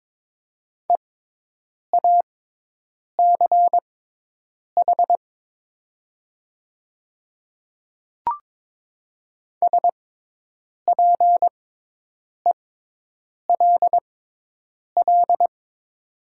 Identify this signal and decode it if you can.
Morse code, heard as a keyed tone, about 22 wpm, spaced as for 8 wpm, 700 Hz. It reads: EACH SPELL
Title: 6x Character Spacing - Top 300 Words - 22wpm